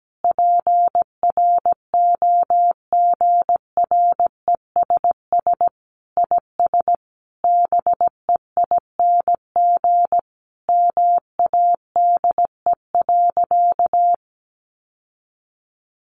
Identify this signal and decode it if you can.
Morse code, heard as a keyed tone, about 17 wpm, 700 Hz. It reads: PROGRESS IS BEING MADE.
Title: Sentences from Top 1400 Words 17wpm